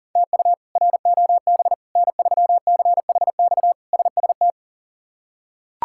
Morse code, heard as a keyed tone, about 40 wpm, 700 Hz. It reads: TU ROB N3CHX SST
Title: K1USN SST QSO Practice - 40wpm